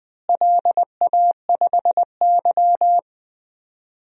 Morse code, heard as a keyed tone, about 20 wpm, 700 Hz. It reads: LA5Y
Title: DXCC-100 Call Signs - 4 Characters - Repeated 2x - 20wpm